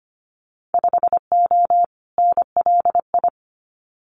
Morse code, heard as a keyed tone, about 25 wpm, 700 Hz. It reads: 5O NLI
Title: Winter Field Day Exchanges 25wpm